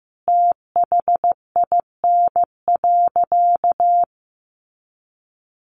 Morse code, heard as a keyed tone, about 14 wpm, 700 Hz. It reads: THIN.
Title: Sentences from Top 1200 Words 15wpm